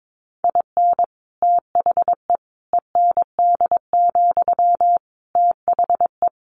Morse code, heard as a keyed tone, about 22 wpm, 700 Hz. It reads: IN THE END, THE
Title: Additional Easy Phrases 22wpm